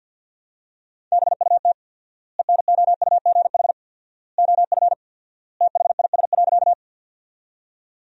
Morse code, heard as a keyed tone, about 50 wpm, 700 Hz. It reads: BUT ENOUGH OF THIS.
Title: 4x Word-Spacing - Sentences from Top 400 Words - 50wpm